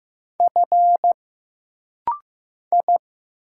Morse code, heard as a keyed tone, about 15 wpm, 700 Hz. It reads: F I
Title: Single Letters - Rapid-Fire 15wpm